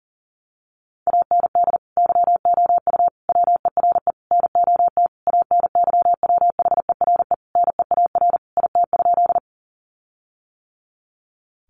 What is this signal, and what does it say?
Morse code, heard as a keyed tone, about 40 wpm, 700 Hz. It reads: AND YOU WERE NOT ANYWHERE NEAR IT?
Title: Sentences from Top 1400 Words 40wpm